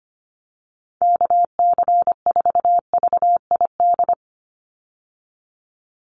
Morse code, heard as a keyed tone, about 25 wpm, 700 Hz. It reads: KC4VID
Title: USA Call Signs 25wpm